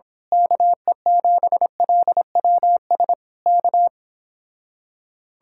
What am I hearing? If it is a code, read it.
Morse code, heard as a keyed tone, about 26 wpm, 700 Hz. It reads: KE7LWS K
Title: Speed-Racing Common QSO Phrases - 22wpm